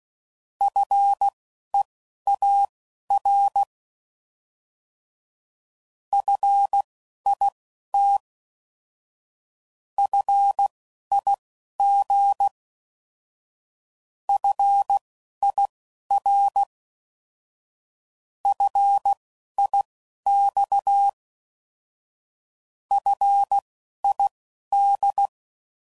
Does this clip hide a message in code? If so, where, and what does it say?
Yes: Morse code from 0.6 to 25.3 s, heard as a keyed tone, about 16 wpm, spaced as for 11 wpm, 800 Hz. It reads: FEAR FIT FIG FIR FIX FID